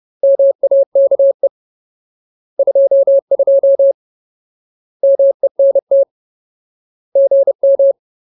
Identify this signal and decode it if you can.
Morse code, heard as a keyed tone, about 30 wpm, 550 Hz. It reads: MAKE 22 MENT GM
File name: Ultimate Word List / 30WPM Ultimate